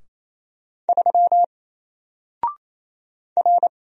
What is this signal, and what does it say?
Morse code, heard as a keyed tone, about 27 wpm, 700 Hz. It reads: 3 R
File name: Single-Letter-Number-28wpm